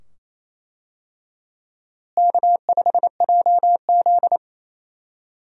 Morse code, heard as a keyed tone, about 28 wpm, 700 Hz. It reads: K5JZ
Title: USA Call Signs - 4 Characters - Repeated 2x - 28wpm